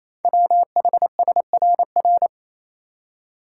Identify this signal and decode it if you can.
Morse code, heard as a keyed tone, about 27 wpm, 700 Hz. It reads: WHSRR
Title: Code Groups - 5 Letters - 28wpm